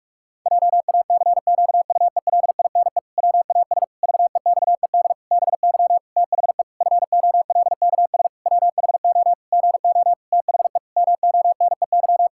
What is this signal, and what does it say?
Morse code, heard as a keyed tone, about 46 wpm, 700 Hz. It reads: JACQUELINE WAS VEXED BY THE FOLKS WHO GO THE MONEY
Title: Pangram Challenge v2 - 45wpm